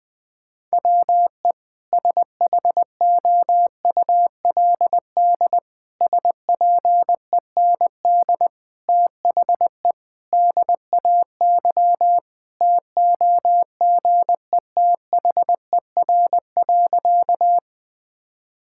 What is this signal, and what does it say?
Morse code, heard as a keyed tone, about 20 wpm, 700 Hz. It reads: WE SHOULD SPEND THE DAY TOGETHER.
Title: Sentences from Top 1000 Words - Review Part 5 - 20wpm